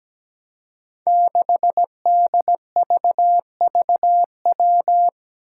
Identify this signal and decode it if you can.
Morse code, heard as a keyed tone, about 17 wpm, 700 Hz. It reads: 6DVVW
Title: Five Challenging Letter-Number Combinations 17wpm